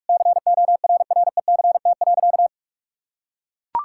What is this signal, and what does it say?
Morse code, heard as a keyed tone, about 42 wpm, 700 Hz. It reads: CORRECT.
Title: Sentences from Top 1000 Words - Review Part 5 - 45wpm